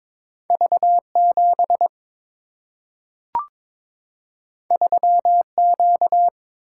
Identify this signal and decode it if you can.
Morse code, heard as a keyed tone, about 22 wpm, 700 Hz. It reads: V7 3Q
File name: practice / 22wpm / Two Letter-Number Combinations - 22wpm